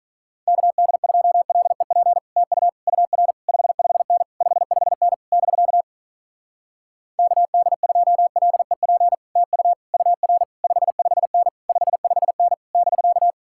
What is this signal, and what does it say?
Morse code, heard as a keyed tone, about 46 wpm, 700 Hz. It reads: KD2LEP TU UR 55N 55N <BK> KD2LEP TU UR 55N 55N <BK>